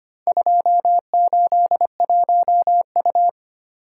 Morse code, heard as a keyed tone, about 25 wpm, 700 Hz. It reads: 281U
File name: practice / 25wpm / Four Letter-Number Combinations - 25wpm